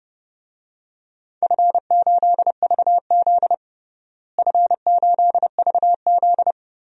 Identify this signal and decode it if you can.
Morse code, heard as a keyed tone, about 30 wpm, 700 Hz. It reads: F8VZ F8VZ
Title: DXCC-100 Call Signs - 4 Characters - Repeated 2x - 30wpm